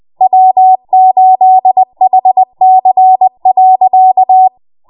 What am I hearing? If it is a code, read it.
Morse code, heard as a keyed tone, about 20 wpm, 750 Hz. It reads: W8HC.
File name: Morse / 20WPM